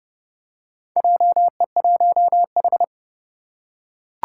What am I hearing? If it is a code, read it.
Morse code, heard as a keyed tone, about 31 wpm, 700 Hz. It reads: JE1H
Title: Four Challenging Letter-Number Combinations 30wpm